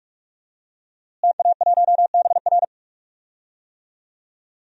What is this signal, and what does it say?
Morse code, heard as a keyed tone, about 45 wpm, 700 Hz. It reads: TA1BR
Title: DXCC-100 Call Signs - 5 Characters - Repeated 2x - 45wpm